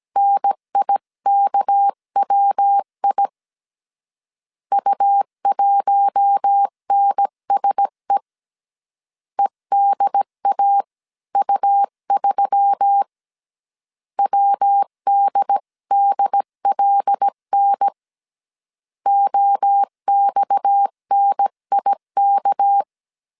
Morse code, heard as a keyed tone, about 17 wpm, 800 Hz. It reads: NIKWI U1NSE EDAU3 WDDLN OXNIK
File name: CW / DET / 1 6 1500C S C 16wpm 2733